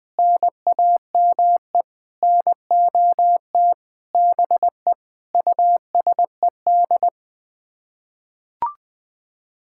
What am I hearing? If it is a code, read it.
Morse code, heard as a keyed tone, about 20 wpm, 700 Hz. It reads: NAME NOT BE USED
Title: Sets of 9 Words - 20wpm